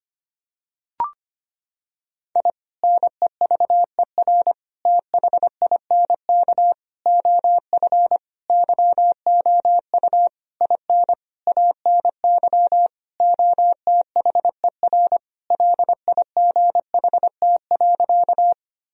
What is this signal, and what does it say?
Morse code, heard as a keyed tone, about 25 wpm, 700 Hz. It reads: I NEVER THINK OF YOU IN ANY OTHER LIGHT.
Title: Sentences from Top 1000 Words - Review Part 1 - 25wpm